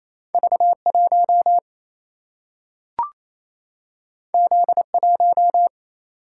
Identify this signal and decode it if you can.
Morse code, heard as a keyed tone, about 28 wpm, 700 Hz. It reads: V 1 Z 1
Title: Single-Letter Single-Number 28wpm